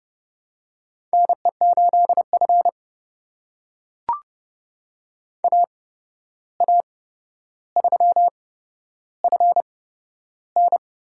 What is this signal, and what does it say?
Morse code, heard as a keyed tone, about 30 wpm, spaced as for 12 wpm, 700 Hz. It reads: NE8F AA3FN
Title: USA Call Signs - 8x Character Spacing - 30wpm